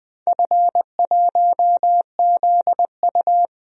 Morse code, heard as a keyed tone, about 20 wpm, 700 Hz. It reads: F1ZU
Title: DXCC-100 Call Signs - 4 Characters - 20wpm